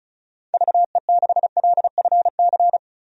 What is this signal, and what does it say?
Morse code, heard as a keyed tone, about 34 wpm, 700 Hz. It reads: VE6LFC
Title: DXCC-100 Call Signs - 6 Characters - 35wpm